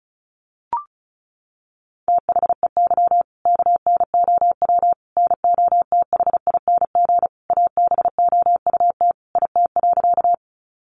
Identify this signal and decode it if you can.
Morse code, heard as a keyed tone, about 34 wpm, 700 Hz. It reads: THEY KNOW NOTHING ABOUT IT.
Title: Sentences from Top 500 Words 35wpm